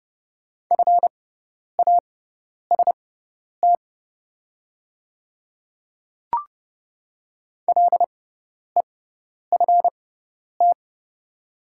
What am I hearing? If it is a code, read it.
Morse code, heard as a keyed tone, about 30 wpm, spaced as for 10 wpm, 700 Hz. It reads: FAST LEFT
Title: 6x Character Spacing - Top 500 Words - 30wpm